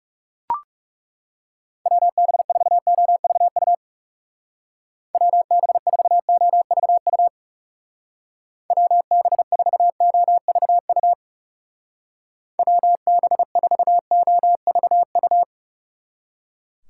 Morse code, heard as a keyed tone, about 38 wpm, 700 Hz. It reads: WB4OVU WB4OVU WB4OVU WB4OVU